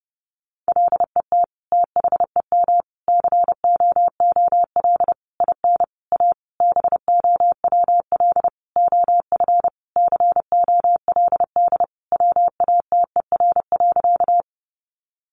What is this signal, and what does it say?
Morse code, heard as a keyed tone, about 30 wpm, 700 Hz. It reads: LET THEM COOL IN A BOWL OF COLD WATER.
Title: Sentences from Top 1200 Words 30wpm